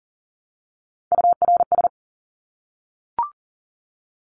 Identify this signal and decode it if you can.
Morse code, heard as a keyed tone, about 40 wpm, 700 Hz. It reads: URS
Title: Abbreviations Repeated 3x - 40wpm